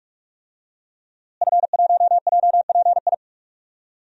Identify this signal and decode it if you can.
Morse code, heard as a keyed tone, about 44 wpm, 700 Hz. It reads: F1JPI
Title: DXCC-100 Call Signs - 5 Characters - Repeated 2x - 45wpm